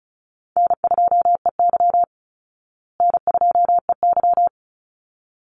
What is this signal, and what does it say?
Morse code, heard as a keyed tone, about 34 wpm, 700 Hz. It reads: N2EY N2EY
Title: Call Signs Repeated 3x - 35wpm